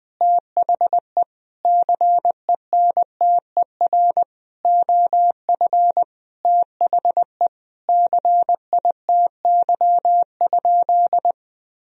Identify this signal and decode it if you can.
Morse code, heard as a keyed tone, about 21 wpm, 700 Hz. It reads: THE CENTER OF THE CITY?
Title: Sentences from Top 1000 Words - Review Part 2 - 20wpm